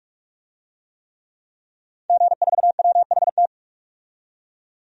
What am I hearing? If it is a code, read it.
Morse code, heard as a keyed tone, about 46 wpm, 700 Hz. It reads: G4WHT
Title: DXCC-100 Call Signs - 5 Characters - 45wpm